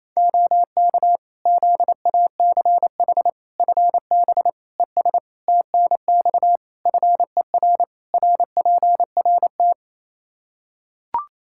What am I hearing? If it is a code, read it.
Morse code, heard as a keyed tone, about 28 wpm, 700 Hz. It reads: OK ZACH FB ES TNX FER RPRT
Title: Practice QSO - Exchange 2 - Info - 28wpm